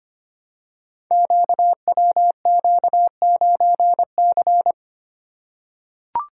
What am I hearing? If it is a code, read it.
Morse code, heard as a keyed tone, about 25 wpm, 700 Hz. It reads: QWQ9C
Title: Code Groups - 5 Letter:Numbers - 25wpm